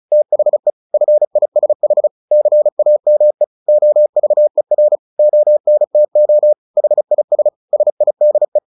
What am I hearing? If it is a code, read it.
Morse code, heard as a keyed tone, about 34 wpm, 600 Hz. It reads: THE FISH CAME OVER ONTO HIS SIDE
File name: the old man and the sea 2 35wpm